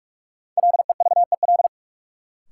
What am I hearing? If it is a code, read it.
Morse code, heard as a keyed tone, about 46 wpm, 700 Hz. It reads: LEVEL
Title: Sentences from Top 700 Words - New Word Drill - 45wpm